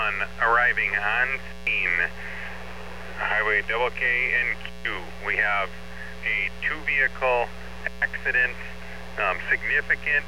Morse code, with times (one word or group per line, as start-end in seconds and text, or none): none